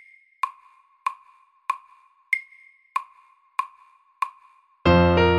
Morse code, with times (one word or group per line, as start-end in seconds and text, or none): none